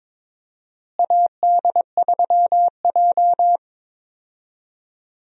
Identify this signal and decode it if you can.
Morse code, heard as a keyed tone, about 22 wpm, 700 Hz. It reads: AD3J
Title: USA Call Signs - 4 Characters - Repeated 3x - 22wpm